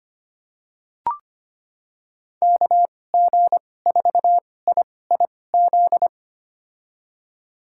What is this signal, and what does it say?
Morse code, heard as a keyed tone, about 25 wpm, spaced as for 18 wpm, 700 Hz. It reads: KG4IIZ
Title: USA Call Signs - 6 Characters - 2x Character Spacing - 25wpm